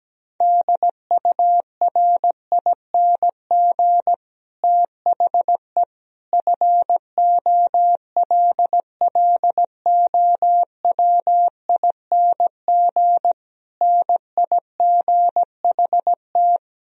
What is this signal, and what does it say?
Morse code, heard as a keyed tone, about 17 wpm, 700 Hz. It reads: DURING THE FOLLOWING NIGHT
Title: Sentences from Top 1000 Words - Review Part 4 - 17wpm